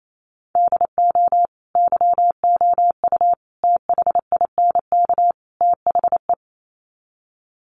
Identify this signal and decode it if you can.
Morse code, heard as a keyed tone, about 28 wpm, 700 Hz. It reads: DO YOU THINK THE
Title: Sets of 4 Words 28wpm